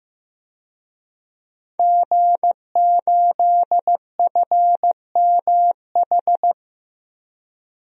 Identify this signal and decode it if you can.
Morse code, heard as a keyed tone, about 15 wpm, 700 Hz. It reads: G8FMH
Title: DXCC-100 Call Signs - 5 Characters - Repeated 3x - 15wpm